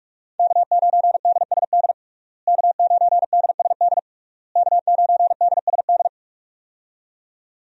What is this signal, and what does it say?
Morse code, heard as a keyed tone, about 46 wpm, 700 Hz. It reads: K9DSD K9DSD K9DSD
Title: USA Call Signs - 5 Characters - Repeated 3x - 45wpm